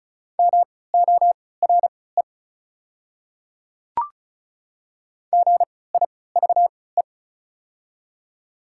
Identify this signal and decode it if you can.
Morse code, heard as a keyed tone, about 36 wpm, spaced as for 20 wpm, 700 Hz. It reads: MORE GIVE
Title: 3x Character Spacing - Top 200 Words - 35wpm